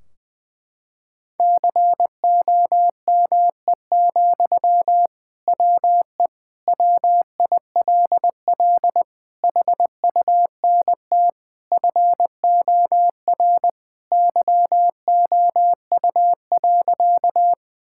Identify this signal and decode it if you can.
Morse code, heard as a keyed tone, about 21 wpm, 700 Hz. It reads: COME, WE WILL HUNT FOR YOU.
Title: Sentences from Top 1600 Words 20wpm